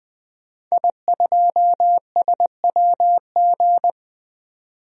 Morse code, heard as a keyed tone, about 20 wpm, 700 Hz. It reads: I2SWG
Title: DXCC-100 Call Signs - 5 Characters - 20wpm